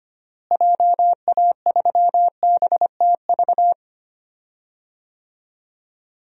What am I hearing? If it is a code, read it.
Morse code, heard as a keyed tone, about 25 wpm, 700 Hz. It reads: JA3BTV
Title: DXCC-100 Call Signs - 6 Characters - Repeated 3x - 25wpm